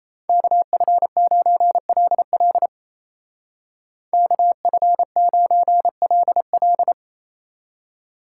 Morse code, heard as a keyed tone, about 35 wpm, 700 Hz. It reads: KF9LL KF9LL
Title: Speed Racing - USA 2x2 Call Signs 28wpm